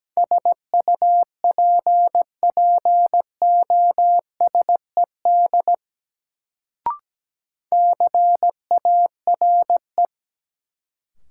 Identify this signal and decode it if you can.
Morse code, heard as a keyed tone, about 17 wpm, 700 Hz. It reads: SUPPOSED CARE
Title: Sets of 3 Words - New Word Drill - 17wpm